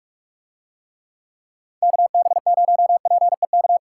Morse code, heard as a keyed tone, about 42 wpm, 700 Hz. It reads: KB0PEK